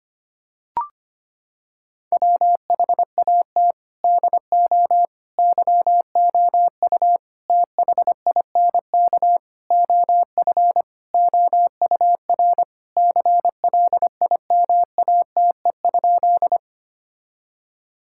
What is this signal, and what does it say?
Morse code, heard as a keyed tone, about 25 wpm, 700 Hz. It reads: WHAT DO YOU THINK OF OUR CLIMATE?
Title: Sentences from Top 1600 Words 25wpm